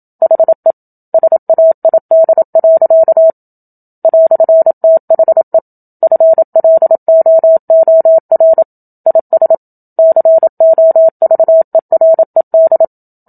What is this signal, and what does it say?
Morse code, heard as a keyed tone, about 27 wpm, 650 Hz. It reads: HE SAID. "THE FLOOR IS COVERED
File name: Book Two / 01-Chapter 1 - 27 WPM